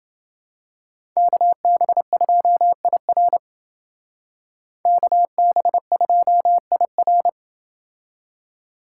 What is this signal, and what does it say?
Morse code, heard as a keyed tone, about 29 wpm, 700 Hz. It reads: KB2IR KB2IR